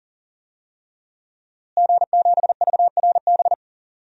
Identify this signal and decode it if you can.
Morse code, heard as a keyed tone, about 40 wpm, 700 Hz. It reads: G7VRB
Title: DXCC-100 Call Signs - 5 Characters - Repeated 3x - 40wpm